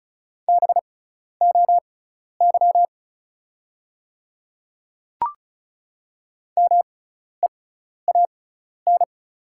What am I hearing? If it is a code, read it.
Morse code, heard as a keyed tone, about 34 wpm, spaced as for 12 wpm, 690 Hz. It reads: BOY MEAN
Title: 6x Character Spacing - Top 200 Words - 35wpm